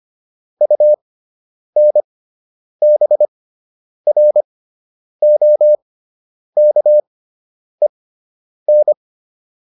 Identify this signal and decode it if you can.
Morse code, heard as a keyed tone, about 25 wpm, spaced as for 9 wpm, 600 Hz. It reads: UNBROKEN